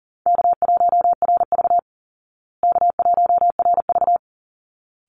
Morse code, heard as a keyed tone, about 40 wpm, 700 Hz. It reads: K1RV K1RV
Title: Call Signs Repeated 3x - 40wpm